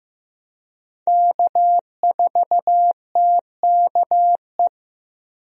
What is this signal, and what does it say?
Morse code, heard as a keyed tone, about 15 wpm, 700 Hz. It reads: K4TKE
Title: USA Call Signs - 5 Characters - Repeated 3x - 15wpm